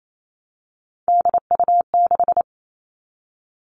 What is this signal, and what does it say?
Morse code, heard as a keyed tone, about 28 wpm, 700 Hz. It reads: DU6
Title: Three Challenging Letter-Number Combinations 28wpm